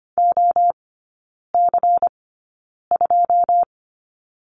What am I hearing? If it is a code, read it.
Morse code, heard as a keyed tone, about 25 wpm, spaced as for 9 wpm, 700 Hz. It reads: OC2
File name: Lesson 17e - Letter u - Three Characters 25wpm